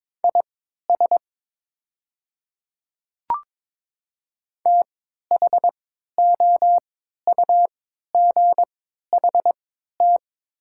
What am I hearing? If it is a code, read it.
Morse code, heard as a keyed tone, about 22 wpm, spaced as for 12 wpm, 700 Hz. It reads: IS THOUGHT
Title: 3x Character Spacing - Top 500 Words - 22wpm